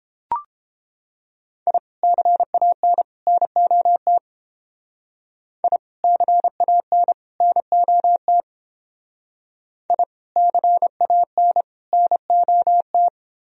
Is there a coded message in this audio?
Yes: Morse code from 1.7 to 13.1 s, heard as a keyed tone, about 31 wpm, 700 Hz. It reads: I CAN NOT I CAN NOT I CAN NOT